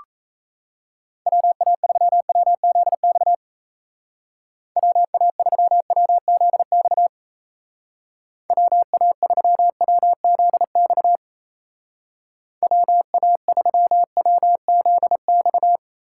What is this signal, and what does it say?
Morse code, heard as a keyed tone, about 33 wpm, 700 Hz. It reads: WA3WZX WA3WZX WA3WZX WA3WZX
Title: Speed Racing - USA 2x3 Call Signs 28wpm